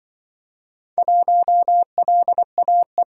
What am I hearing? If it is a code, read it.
Morse code, heard as a keyed tone, about 24 wpm, 700 Hz. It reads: 1LAE
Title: Speed-Racing - Four Letter-Number Combinations - 20wpm